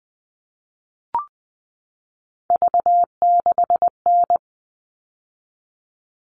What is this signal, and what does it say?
Morse code, heard as a keyed tone, about 20 wpm, 700 Hz. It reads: V6N
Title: All Cut Numbers - 3-Digit 20wpm